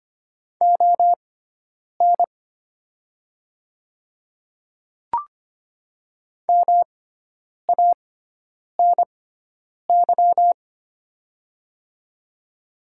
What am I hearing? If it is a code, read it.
Morse code, heard as a keyed tone, about 25 wpm, spaced as for 9 wpm, 700 Hz. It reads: ON MANY
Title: 6x Character Spacing - Top 300 Words - 25wpm